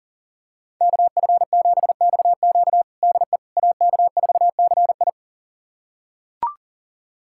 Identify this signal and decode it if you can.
Morse code, heard as a keyed tone, about 37 wpm, 700 Hz. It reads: KF7XQ DE AK4CI
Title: USA Callsign DE Callsign - 5 Characters - 40wpm